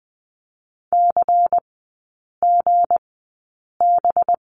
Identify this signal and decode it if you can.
Morse code, heard as a keyed tone, about 20 wpm, spaced as for 8 wpm, 700 Hz. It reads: CGB
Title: 212 Lesson 34e - Number 8 - Three Characters 20wpm